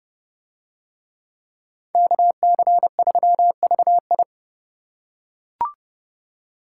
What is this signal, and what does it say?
Morse code, heard as a keyed tone, about 29 wpm, 700 Hz. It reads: KC3VI